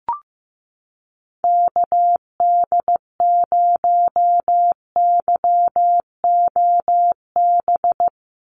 Morse code, heard as a keyed tone, about 15 wpm, 700 Hz. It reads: KD0YOB
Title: NA SOTA Activator Callsigns 15wpm